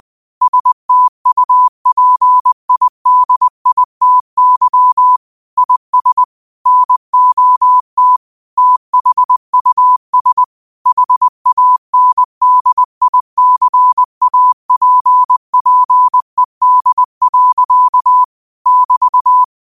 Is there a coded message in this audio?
Yes: Morse code from 0.4 to 19.4 s, heard as a keyed tone, about 20 wpm, 1000 Hz. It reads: STUPIDITY IS NOT THUS HANDICAPPED. <BT>